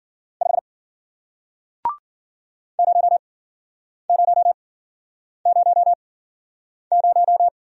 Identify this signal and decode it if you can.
Morse code, heard as a keyed tone, about 54 wpm, spaced as for 9 wpm, 700 Hz. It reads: 5 0000